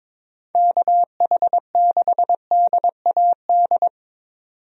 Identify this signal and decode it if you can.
Morse code, heard as a keyed tone, about 22 wpm, 700 Hz. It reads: KH6DAD
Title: DXCC-100 Call Signs - 6 Characters - 22wpm